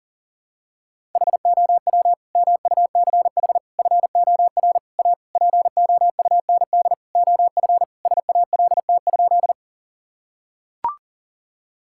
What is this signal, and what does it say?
Morse code, heard as a keyed tone, about 40 wpm, 700 Hz. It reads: HOW MUCH FOR A POUND OF SALT?
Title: Sentences from Top 1000 Words - Review Part 5 - 40wpm